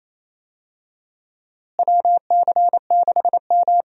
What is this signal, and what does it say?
Morse code, heard as a keyed tone, about 28 wpm, 700 Hz. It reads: WC6M
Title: USA Call Signs - 4 Characters - Repeated 2x - 28wpm